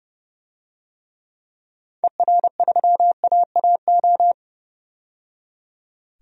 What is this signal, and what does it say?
Morse code, heard as a keyed tone, about 30 wpm, 700 Hz. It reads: ER3AAO
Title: DXCC-100 Call Signs - 6 Characters - 30wpm